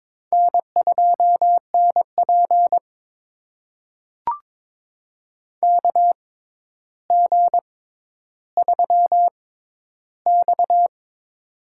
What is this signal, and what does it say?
Morse code, heard as a keyed tone, about 22 wpm, spaced as for 11 wpm, 700 Hz. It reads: N2NP KG3X
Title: USA Call Signs - 4 Characters - 6x Character Spacing - 22wpm